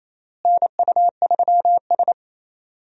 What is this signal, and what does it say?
Morse code, heard as a keyed tone, about 28 wpm, 700 Hz. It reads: NU3S
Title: USA Call Signs - 4 Characters - Repeated 3x - 28wpm